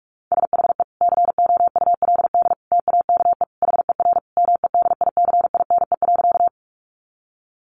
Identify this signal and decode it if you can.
Morse code, heard as a keyed tone, about 44 wpm, 700 Hz. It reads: SHE COULD TAKE HER MEDICINE.
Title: Sentences from Top 900 Words 45wpm